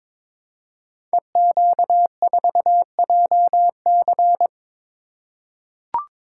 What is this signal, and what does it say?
Morse code, heard as a keyed tone, about 23 wpm, 700 Hz. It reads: EQ4JC
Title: Code Groups - 5 Letter:Numbers - 22wpm